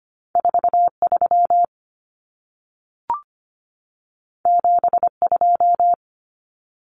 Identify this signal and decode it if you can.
Morse code, heard as a keyed tone, about 25 wpm, 700 Hz. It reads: 43 72
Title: Common Cut Numbers - 2-Digit 25wpm